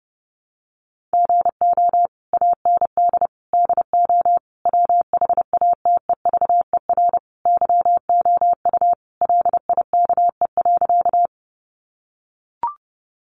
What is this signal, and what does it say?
Morse code, heard as a keyed tone, about 30 wpm, 700 Hz. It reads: GO AND DO WHATEVER YOU LIKE.
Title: Sentences from Top 1100 Words 30wpm